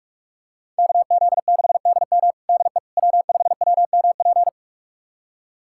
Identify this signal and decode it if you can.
Morse code, heard as a keyed tone, about 46 wpm, 700 Hz. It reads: KZ6DM DE W5WMP
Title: USA Callsign DE Callsign - 5 Characters - 45wpm